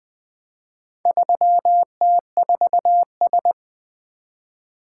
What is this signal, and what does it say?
Morse code, heard as a keyed tone, about 20 wpm, 700 Hz. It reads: 3T4S
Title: Four Letter-Number Combinations - 20wpm